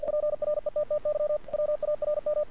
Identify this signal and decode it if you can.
Morse code, heard as a keyed tone, about 47 wpm, 600 Hz. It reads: PRETTY WARM
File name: archived / CRSnet